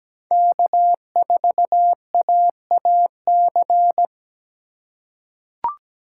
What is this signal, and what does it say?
Morse code, heard as a keyed tone, about 17 wpm, 700 Hz. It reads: K4AAC
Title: K1USN SST QSO Practice - 17wpm